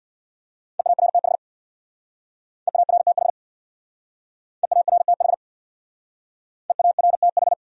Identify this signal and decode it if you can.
Morse code, heard as a keyed tone, about 69 wpm, 700 Hz. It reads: EARTH EARTH EARTH EARTH